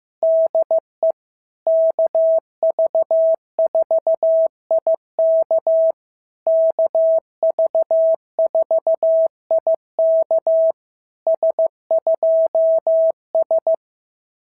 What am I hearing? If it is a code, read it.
Morse code, heard as a keyed tone, about 15 wpm, 690 Hz. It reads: DE KV4IK KV4IK S2S
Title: SOTA QSO Practice 15wpm